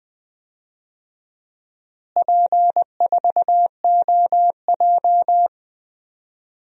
Morse code, heard as a keyed tone, about 20 wpm, 700 Hz. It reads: P4OJ
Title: DXCC-100 Call Signs - 4 Characters - Repeated 2x - 20wpm